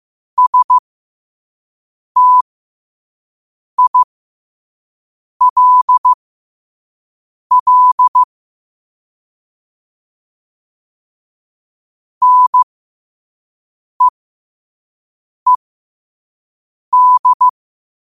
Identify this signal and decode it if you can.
Morse code, heard as a keyed tone, about 15 wpm, spaced as for 5 wpm, 1000 Hz. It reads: STILL NEED
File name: 5wpm